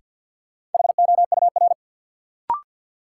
Morse code, heard as a keyed tone, about 50 wpm, 700 Hz. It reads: HOUR
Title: Sentences from Top 400 Words - New Word Drill - 50wpm